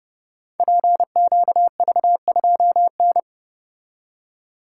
Morse code, heard as a keyed tone, about 30 wpm, 700 Hz. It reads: PQV2N
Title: Code Groups - 5 Letter:Numbers - 30wpm